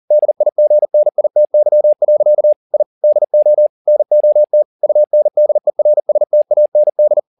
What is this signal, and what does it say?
Morse code, heard as a keyed tone, about 40 wpm, 600 Hz. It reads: DIGNITY. I DO NOT UNDERSTAND
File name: the old man and the sea 2 40wpm